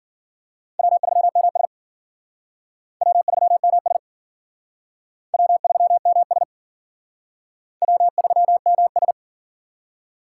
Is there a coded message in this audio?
Yes: Morse code from 0.8 to 9.1 s, heard as a keyed tone, about 52 wpm, 700 Hz. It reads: W3MS W3MS W3MS W3MS